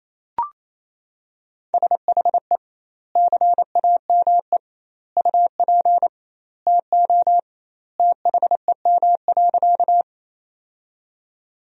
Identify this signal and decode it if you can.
Morse code, heard as a keyed tone, about 28 wpm, 700 Hz. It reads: SHE CAME UP TO THEM.
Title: 2x Word-Spacing - Sentences from Top 200 Words - 28wpm